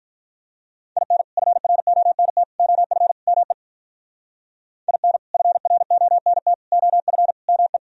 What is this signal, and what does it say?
Morse code, heard as a keyed tone, about 43 wpm, 700 Hz. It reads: IN FRONT OF ME IN FRONT OF ME
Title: Speed-Racing - Sets of 4 Words 35wpm